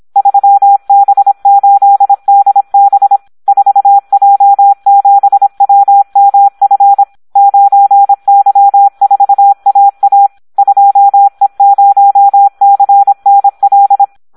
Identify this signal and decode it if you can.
Morse code, heard as a keyed tone, about 26 wpm, 800 Hz, with radio noise. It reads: 3B8DB 4J7WMF 9Y4AA 2E0CNL
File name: Audio practica CW / SD 1 26